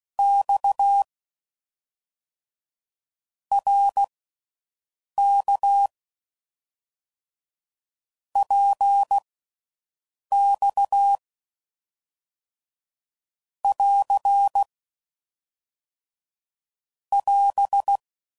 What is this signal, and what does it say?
Morse code, heard as a keyed tone, about 16 wpm, spaced as for 6 wpm, 800 Hz. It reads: X RK PX <AR> <AS>